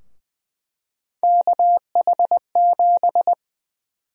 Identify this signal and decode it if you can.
Morse code, heard as a keyed tone, about 20 wpm, 700 Hz. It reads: KH7
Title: USA Call Sign Prefixes - 20wpm